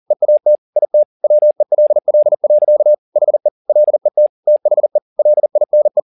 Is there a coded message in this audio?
Yes: Morse code from 0.1 to 6.0 s, heard as a keyed tone, about 40 wpm, 600 Hz. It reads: EAT IT WELL. HE LET THE LINE